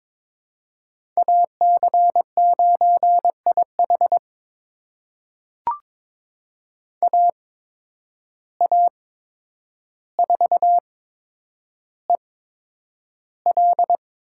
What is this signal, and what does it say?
Morse code, heard as a keyed tone, about 22 wpm, spaced as for 9 wpm, 700 Hz. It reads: AC9IH AA4EL
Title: USA Call Signs - 5 Characters - 8x Character Spacing - 22wpm